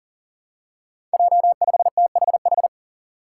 Morse code, heard as a keyed tone, about 40 wpm, 700 Hz. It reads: J5THH
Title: Five Challenging Letter-Number Combinations 40wpm